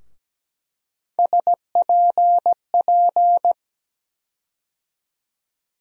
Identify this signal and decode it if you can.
Morse code, heard as a keyed tone, about 17 wpm, 700 Hz. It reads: SPP
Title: USA Call Sign Postfixes - 3 Letters 17wpm